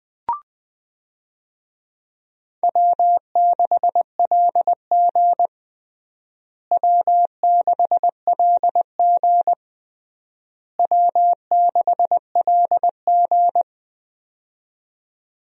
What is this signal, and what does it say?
Morse code, heard as a keyed tone, about 20 wpm, 700 Hz. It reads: W6LG W6LG W6LG
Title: USA Call Signs - 4 Characters - Repeated 3x - 20wpm